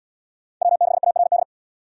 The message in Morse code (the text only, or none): W6ARD